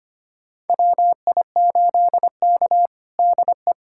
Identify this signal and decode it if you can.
Morse code, heard as a keyed tone, about 25 wpm, 700 Hz. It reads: WI8K DE
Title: USA Callsign DE Callsign 25wpm